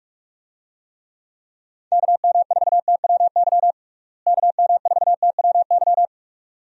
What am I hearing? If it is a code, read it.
Morse code, heard as a keyed tone, about 41 wpm, 700 Hz. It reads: KM4TWY KM4TWY
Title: USA Call Signs - 6 Characters - Repeated 2x - 45wpm